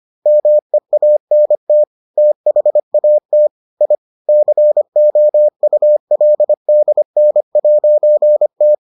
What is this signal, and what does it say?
Morse code, heard as a keyed tone, about 25 wpm, 600 Hz. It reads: MEANT THAT I COULDN'T